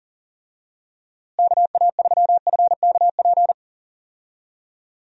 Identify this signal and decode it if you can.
Morse code, heard as a keyed tone, about 40 wpm, 700 Hz. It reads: KA3FKP
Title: USA Call Signs - 6 Characters - Repeated 3x - 40wpm